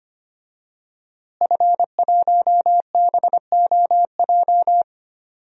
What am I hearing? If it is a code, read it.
Morse code, heard as a keyed tone, about 25 wpm, 700 Hz. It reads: F1BOJ
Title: DXCC-100 Call Signs - 5 Characters - Repeated 3x - 25wpm